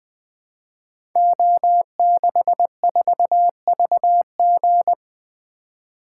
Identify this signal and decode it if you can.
Morse code, heard as a keyed tone, about 20 wpm, 700 Hz. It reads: O64VG
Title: Code Groups - 5 Letter:Numbers - 20wpm